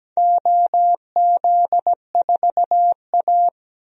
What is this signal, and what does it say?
Morse code, heard as a keyed tone, about 17 wpm, 700 Hz. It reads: OZ4A
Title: DXCC-100 Call Signs - 4 Characters - 17wpm